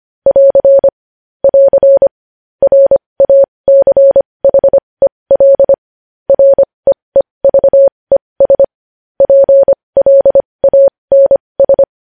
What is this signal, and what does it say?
Morse code, heard as a keyed tone, about 25 wpm, spaced as for 20 wpm, 550 Hz. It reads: <AR> <AR> RACHEL REEVES PLANS
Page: News Headlines in Morse Code at 20 WPM